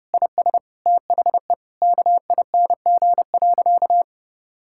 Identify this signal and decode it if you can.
Morse code, heard as a keyed tone, about 32 wpm, 700 Hz. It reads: IS THE KING.
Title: Sentences from Top 1600 Words 30wpm